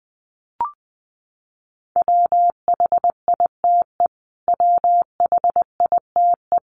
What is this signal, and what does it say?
Morse code, heard as a keyed tone, about 20 wpm, 700 Hz. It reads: WHITE WHITE
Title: Top 300 Words Repeated 20wpm